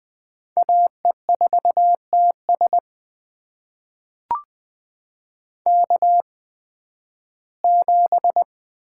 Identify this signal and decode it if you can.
Morse code, heard as a keyed tone, about 20 wpm, spaced as for 13 wpm, 700 Hz. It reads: AE4TS K7
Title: USA Call Signs - 5 Characters - 8x Character Spacing - 20wpm